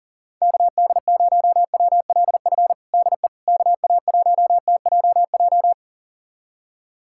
Morse code, heard as a keyed tone, about 40 wpm, 700 Hz. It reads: KD0WLF DE KA1TJJ
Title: USA Callsign DE Callsign - 6 Characters - 40wpm